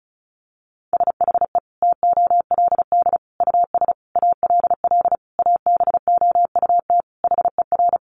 Morse code, heard as a keyed tone, about 34 wpm, 700 Hz. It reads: SHE TOLD US ALL ABOUT HER
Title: Sentences from Top 1200 Words 35wpm